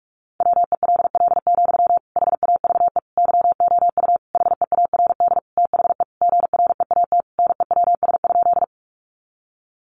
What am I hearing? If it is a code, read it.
Morse code, heard as a keyed tone, about 44 wpm, 700 Hz. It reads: WELL, HAVE YOU HEARD THE GREAT NEWS?
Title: Sentences from Top 800 Words 45wpm